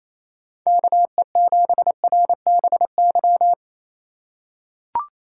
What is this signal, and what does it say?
Morse code, heard as a keyed tone, about 28 wpm, 700 Hz. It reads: KE7RBY
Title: USA Call Signs - 6 Characters - 4x Character Spacing - 28wpm